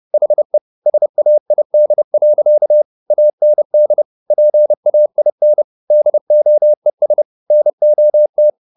Morse code, heard as a keyed tone, about 30 wpm, 600 Hz. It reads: HE SAID. AND PAIN DOES NOT